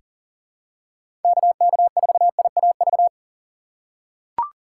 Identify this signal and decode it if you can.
Morse code, heard as a keyed tone, about 38 wpm, 700 Hz. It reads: KK4IAV
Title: USA Call Signs - 6 Characters - 4x Character Spacing - 40wpm